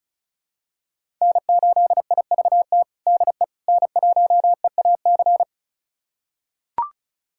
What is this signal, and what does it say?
Morse code, heard as a keyed tone, about 34 wpm, 700 Hz. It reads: N8IVT DE N1EAC
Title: USA Callsign DE Callsign - 5 Characters - 35wpm